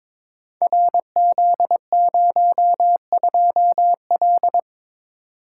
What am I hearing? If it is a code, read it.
Morse code, heard as a keyed tone, about 22 wpm, 700 Hz. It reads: RZ02L
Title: Code Groups - 5 Letter:Numbers - 22wpm